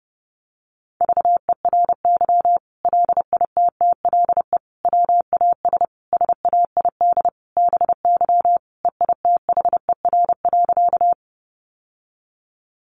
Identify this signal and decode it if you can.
Morse code, heard as a keyed tone, about 30 wpm, 700 Hz. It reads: VERY LITTLE WAS SAID BY EITHER.
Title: Sentences from Top 600 Words 30wpm